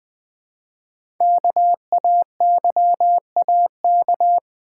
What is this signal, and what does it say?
Morse code, heard as a keyed tone, about 20 wpm, 700 Hz. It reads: KAYAK
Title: Easy Words with Hints - 20wpm